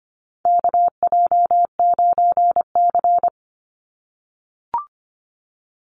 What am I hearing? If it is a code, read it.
Morse code, heard as a keyed tone, about 25 wpm, 700 Hz. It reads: KJ9C
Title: Call Signs - 25wpm